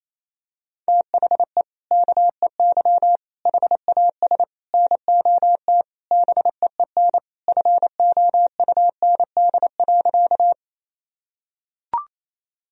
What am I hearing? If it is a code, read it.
Morse code, heard as a keyed tone, about 28 wpm, 700 Hz. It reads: THE KEY HAS NOT BEEN FOUND.
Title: Sentences from Top 1000 Words - Review Part 4 - 28wpm